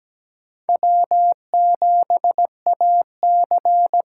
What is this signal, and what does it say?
Morse code, heard as a keyed tone, about 17 wpm, 700 Hz. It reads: W7AC
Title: USA Call Signs - 5 Characters - 8x Character Spacing - 17wpm